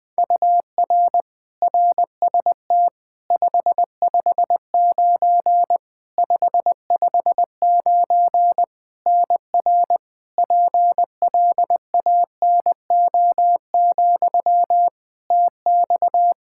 Morse code, heard as a keyed tone, about 20 wpm, 700 Hz. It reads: UR RST 559 559 NR PLANO, TX
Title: Practice QSO - Exchange 1 - Beginning - 20wpm